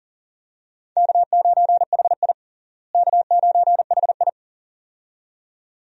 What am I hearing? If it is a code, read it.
Morse code, heard as a keyed tone, about 40 wpm, 700 Hz. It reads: K9HI K9HI
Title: USA Call Signs - 4 Characters - Repeated 2x - 40wpm